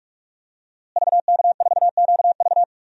Morse code, heard as a keyed tone, about 46 wpm, 700 Hz. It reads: VK4QV
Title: DXCC-100 Call Signs - 5 Characters - Repeated 3x - 45wpm